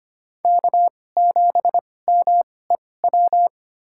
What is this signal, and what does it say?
Morse code, heard as a keyed tone, about 25 wpm, spaced as for 18 wpm, 700 Hz. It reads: K7MEW
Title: USA Call Signs - 2x Character Spacing - 25wpm